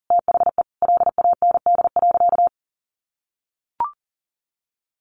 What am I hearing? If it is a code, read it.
Morse code, heard as a keyed tone, about 40 wpm, 700 Hz. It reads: THE LAND.
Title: Sentences from Top 1200 Words 40wpm